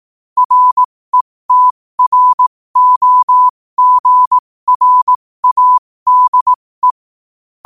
Morse code, heard as a keyed tone, about 18 wpm, spaced as for 15 wpm, 1000 Hz. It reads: RETROGRADE
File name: qotdfiles / 15wpm